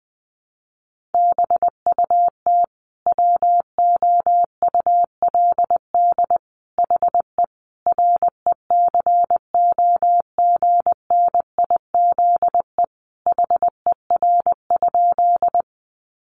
Morse code, heard as a keyed tone, about 20 wpm, 700 Hz. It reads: BUT WOULD HE RECOGNIZE HER?
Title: Sentences from Top 1400 Words 20wpm